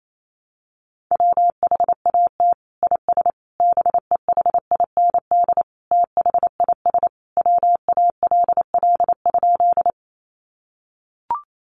Morse code, heard as a keyed tone, about 28 wpm, 700 Hz. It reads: WHAT IS BEHIND THIS WALL?
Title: Sentences from Top 1100 Words 28wpm